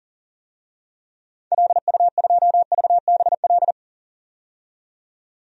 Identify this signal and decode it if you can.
Morse code, heard as a keyed tone, about 40 wpm, 700 Hz. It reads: LU2VBL
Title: DXCC-100 Call Signs - 6 Characters - 40wpm